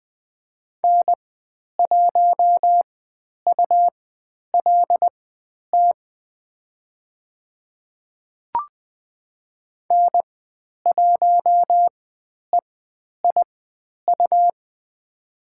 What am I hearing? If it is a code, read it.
Morse code, heard as a keyed tone, about 20 wpm, spaced as for 10 wpm, 700 Hz. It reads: N1ULT N1EIU
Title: 109 Lesson 18g - Letter m - Call Signs 20wpm